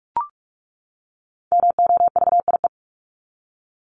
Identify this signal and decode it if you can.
Morse code, heard as a keyed tone, about 44 wpm, 700 Hz. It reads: MOVIE
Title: Top 501-750 Words 45wpm